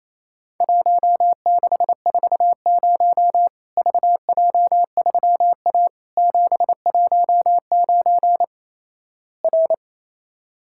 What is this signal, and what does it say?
Morse code, heard as a keyed tone, about 28 wpm, 700 Hz. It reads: 1640 VJ3A 719 R
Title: Worked All Europe DX Contest - QTC Exchange - 28wpm